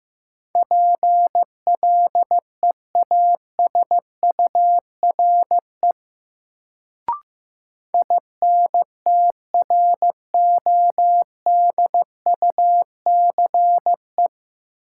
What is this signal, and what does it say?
Morse code, heard as a keyed tone, about 15 wpm, 700 Hz. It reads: PLEASURE INTRODUCE
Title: Sentences from Top 1600 Words - New Word Drill Alt - 15wpm